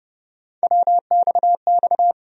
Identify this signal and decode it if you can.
Morse code, heard as a keyed tone, about 31 wpm, 700 Hz. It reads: WXX
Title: USA Call Sign Postfixes - 3 Letters 30wpm